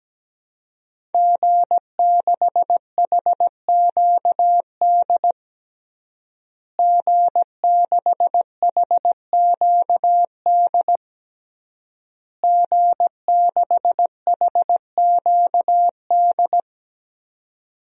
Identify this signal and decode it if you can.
Morse code, heard as a keyed tone, about 17 wpm, 700 Hz. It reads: G6HQD G6HQD G6HQD
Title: DXCC-100 Call Signs - 5 Characters - Repeated 3x - 17wpm